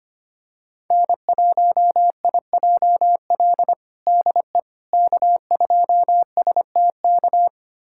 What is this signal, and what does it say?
Morse code, heard as a keyed tone, about 25 wpm, 700 Hz. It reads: N1IJL DE K2STK
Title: USA Callsign DE Callsign - 5 Characters - 25wpm